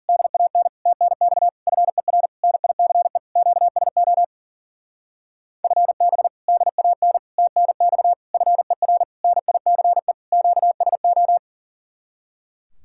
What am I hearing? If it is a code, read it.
Morse code, heard as a keyed tone, about 52 wpm, 700 Hz. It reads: DAN TNX FER NICE QSO FB DAN TNX FER NICE QSO